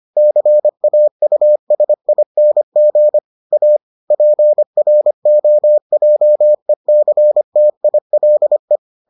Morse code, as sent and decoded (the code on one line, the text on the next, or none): -.-. .- ..- ... .. -. --. / .- / .--. .-. --- .--- . -.-. - .. .-.. .
CAUSING A PROJECTILE